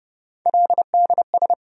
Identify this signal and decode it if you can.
Morse code, heard as a keyed tone, about 30 wpm, 700 Hz. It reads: LDS
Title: USA Call Sign Postfixes - 3 Letters 30wpm